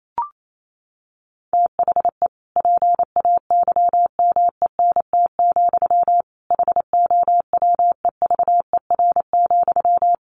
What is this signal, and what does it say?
Morse code, heard as a keyed tone, about 28 wpm, 700 Hz. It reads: THE PAYMENT, HOWEVER,
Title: Sentences from Top 1200 Words 28wpm